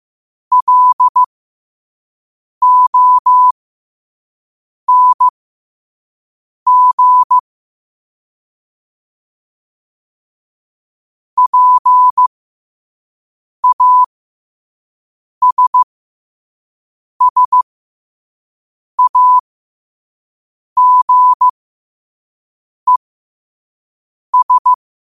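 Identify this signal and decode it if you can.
Morse code, heard as a keyed tone, about 15 wpm, spaced as for 5 wpm, 1000 Hz. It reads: LONG PASSAGES
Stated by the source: Morse code practice at 5 wpm.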